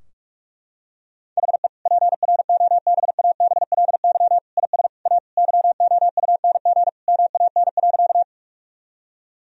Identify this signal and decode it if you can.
Morse code, heard as a keyed tone, about 42 wpm, 700 Hz. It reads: HE PROBABLY IS A YOUNG MAN.
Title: Sentences from Top 1000 Words - Review Part 4 - 45wpm